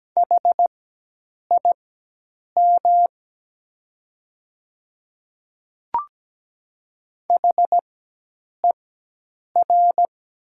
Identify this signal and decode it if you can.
Morse code, heard as a keyed tone, about 17 wpm, spaced as for 8 wpm, 700 Hz. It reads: HIM HER
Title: 4x Character Spacing - Top 100 Words - 17wpm